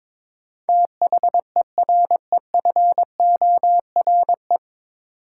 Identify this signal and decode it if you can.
Morse code, heard as a keyed tone, about 22 wpm, 700 Hz. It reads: THEREFORE
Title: Sentences from Top 800 Words - New Word Drill - 22wpm